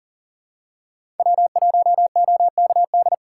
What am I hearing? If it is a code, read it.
Morse code, heard as a keyed tone, about 40 wpm, 700 Hz. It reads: W1OKD
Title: USA Call Signs - 4x Character Spacing - 40wpm